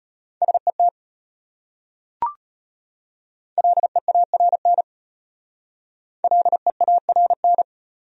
Morse code, heard as a keyed tone, about 33 wpm, 700 Hz. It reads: SET LEARN LEARN